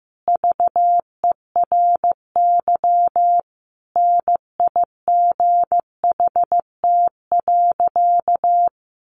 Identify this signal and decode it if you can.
Morse code, heard as a keyed tone, about 15 wpm, 700 Hz. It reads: VERY NIGHT.
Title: Sentences from Top 1200 Words 15wpm